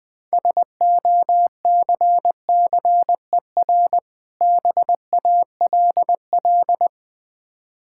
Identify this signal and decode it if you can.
Morse code, heard as a keyed tone, about 20 wpm, 700 Hz. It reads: SOCCER BALL
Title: Easy Words - Speed-Racing with Hints - 20wpm